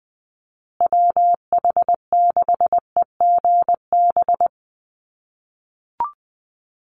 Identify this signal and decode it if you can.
Morse code, heard as a keyed tone, about 20 wpm, 700 Hz. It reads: WH6EGB
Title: USA Call Signs Repeated 2x 20wpm